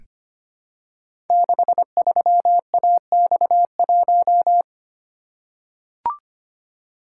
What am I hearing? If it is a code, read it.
Morse code, heard as a keyed tone, about 25 wpm, 700 Hz. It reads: 63AX1
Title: Code Groups - 5 Letter:Numbers - 25wpm